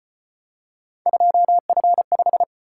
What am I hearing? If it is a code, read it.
Morse code, heard as a keyed tone, about 32 wpm, 700 Hz. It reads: 2F5